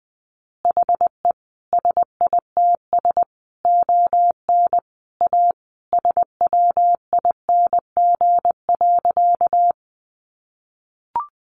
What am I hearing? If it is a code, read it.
Morse code, heard as a keyed tone, about 19 wpm, 700 Hz. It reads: HE SITS ON A SWING.
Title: Additional Easy Phrases 20wpm